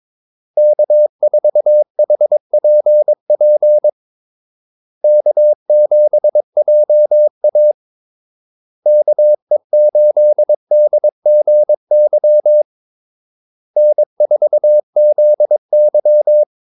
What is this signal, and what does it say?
Morse code, heard as a keyed tone, about 22 wpm, 600 Hz. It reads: K4HPP K7JA KE8DGY N4ZY
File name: Callsigns-22wpm